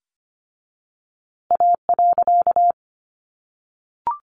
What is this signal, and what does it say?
Morse code, heard as a keyed tone, about 25 wpm, 700 Hz. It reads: A.
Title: A Sound Of Thunder 25wpm